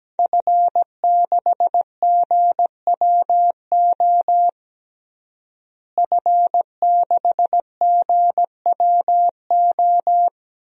Morse code, heard as a keyed tone, about 17 wpm, 700 Hz. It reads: F6GWO F6GWO
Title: DXCC-100 Call Signs - 5 Characters - Repeated 3x - 17wpm